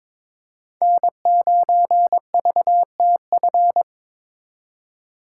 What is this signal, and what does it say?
Morse code, heard as a keyed tone, about 26 wpm, 700 Hz. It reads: N9VTF
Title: Speed Racing - USA 1x3 Call Signs 22wpm